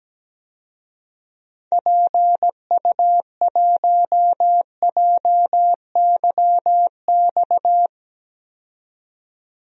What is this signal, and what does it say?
Morse code, heard as a keyed tone, about 17 wpm, 700 Hz. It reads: PU1JYX
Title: DXCC-100 Call Signs - 6 Characters - 17wpm